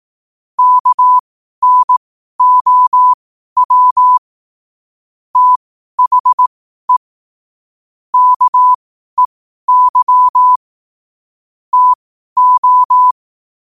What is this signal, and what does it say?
Morse code, heard as a keyed tone, about 18 wpm, spaced as for 12 wpm, 1000 Hz. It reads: KNOW THE KEY TO
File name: qotdfiles / 12wpm